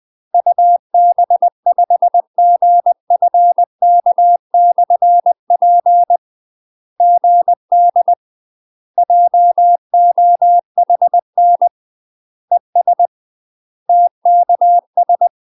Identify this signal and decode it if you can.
Morse code, heard as a keyed tone, about 20 wpm, 700 Hz. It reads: UB5GFK/P GD JOHN ES TKS
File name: UK MASTER QSOs TRAINING / 20WPM / UK MASTER QSOs 0044